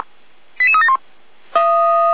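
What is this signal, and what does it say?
Five-tone selective call (ZVEI), heard as quick runs of five tones, tones of about 70 ms. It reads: 88371
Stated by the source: ZVEI five-tone selective call received on an SDR, archived by feuerwehr-orlamuende.org.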